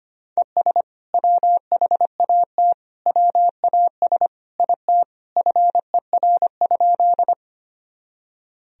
Morse code, heard as a keyed tone, about 25 wpm, 700 Hz. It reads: ES WHAT WAS IT FER?
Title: Sentences from Top 100 Words and Abbreviations 25wpm